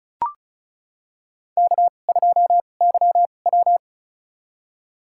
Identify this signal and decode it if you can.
Morse code, heard as a keyed tone, about 35 wpm, spaced as for 25 wpm, 700 Hz. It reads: K2YW
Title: USA Call Signs - 4 Characters - 2x Character Spacing - 35wpm